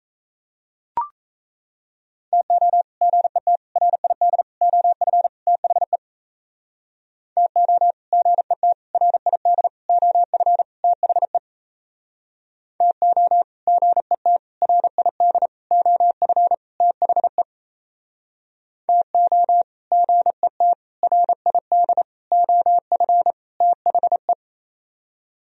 Morse code, heard as a keyed tone, about 35 wpm, 700 Hz. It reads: TO GET RID OF THE TO GET RID OF THE TO GET RID OF THE TO GET RID OF THE